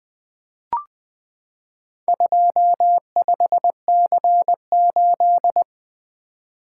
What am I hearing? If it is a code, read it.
Morse code, heard as a keyed tone, about 20 wpm, 700 Hz. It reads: 25C8
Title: Four Letter-Number Combinations - 20wpm